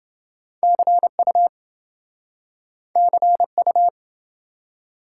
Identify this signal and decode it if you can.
Morse code, heard as a keyed tone, about 27 wpm, 700 Hz. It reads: CU CU